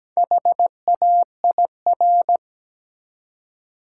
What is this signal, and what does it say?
Morse code, heard as a keyed tone, about 17 wpm, 700 Hz. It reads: HAIR
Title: Four Letter Words - 17wpm